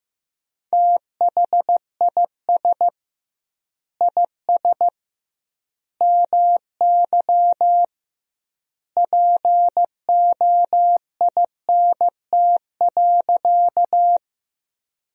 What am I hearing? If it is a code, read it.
Morse code, heard as a keyed tone, about 15 wpm, 700 Hz. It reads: THIS IS MY POINT.
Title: 2x Word-Spacing - Sentences from Top 200 Words - 15wpm